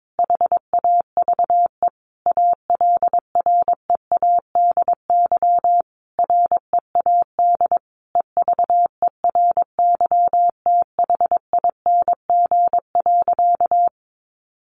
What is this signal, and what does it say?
Morse code, heard as a keyed tone, about 22 wpm, 700 Hz. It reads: HAVE ALREADY READ EVERYTHING.
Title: Sentences from Top 700 Words 22wpm